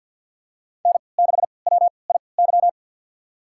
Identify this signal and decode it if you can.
Morse code, heard as a keyed tone, about 51 wpm, spaced as for 28 wpm, 700 Hz. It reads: N6WIY
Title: USA Call Signs - 5 Characters - 3x Character Spacing - 50wpm